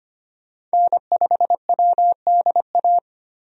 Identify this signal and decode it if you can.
Morse code, heard as a keyed tone, about 25 wpm, 700 Hz. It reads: N5WDA
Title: USA Call Signs - 5 Characters - 25wpm